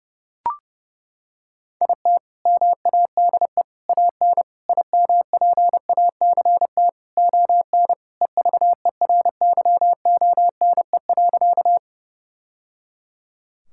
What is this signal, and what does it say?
Morse code, heard as a keyed tone, about 32 wpm, 700 Hz. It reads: IT MADE AN IMPACT ON EVERYONE.
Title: Sentences from Top 1000 Words - Review Part 6 - 30wpm